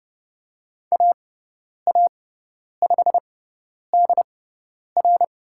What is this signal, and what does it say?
Morse code, heard as a keyed tone, about 30 wpm, spaced as for 10 wpm, 700 Hz. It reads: AA5DR
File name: Lesson 18g - Letter m - Call Signs 30wpm